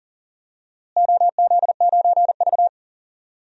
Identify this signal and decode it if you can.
Morse code, heard as a keyed tone, about 38 wpm, 700 Hz. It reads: OZ9V
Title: DXCC-100 Call Signs - 4 Characters - Repeated 2x - 40wpm